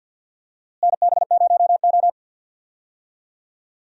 Morse code, heard as a keyed tone, about 52 wpm, 700 Hz. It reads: NB0O
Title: USA Call Signs - 4 Characters - Repeated 2x - 50wpm